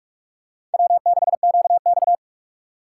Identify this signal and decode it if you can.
Morse code, heard as a keyed tone, about 46 wpm, 700 Hz. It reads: W6QX
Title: USA Call Signs - 4 Characters - 6x Character Spacing - 45wpm